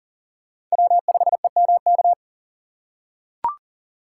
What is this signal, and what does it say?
Morse code, heard as a keyed tone, about 40 wpm, 700 Hz. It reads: W5EMK
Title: USA Call Signs - 5 Characters - 4x Character Spacing - 40wpm